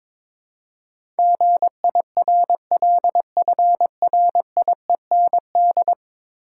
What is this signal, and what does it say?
Morse code, heard as a keyed tone, about 22 wpm, 700 Hz. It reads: GIRLFRIEND
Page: morse-code-ninja-practice-22wpm